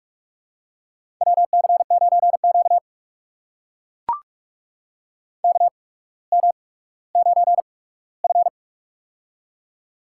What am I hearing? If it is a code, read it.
Morse code, heard as a keyed tone, about 44 wpm, spaced as for 19 wpm, 700 Hz. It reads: WC9Q KM9F
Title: USA Call Signs - 4 Characters - 8x Character Spacing - 45wpm